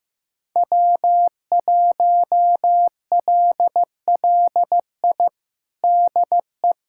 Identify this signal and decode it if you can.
Morse code, heard as a keyed tone, about 15 wpm, 700 Hz. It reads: W1LLI DE
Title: USA Callsign DE Callsign - 5 Characters - 15wpm